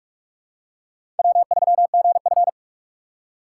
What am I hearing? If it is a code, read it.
Morse code, heard as a keyed tone, about 47 wpm, 700 Hz. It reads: W3GF
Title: USA Call Signs - 4 Characters - 6x Character Spacing - 45wpm